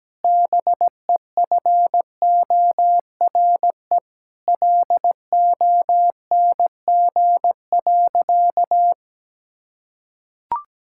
Morse code, heard as a keyed tone, about 17 wpm, 700 Hz. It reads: BEFORE LONG.
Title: Sentences from Top 1600 Words 17wpm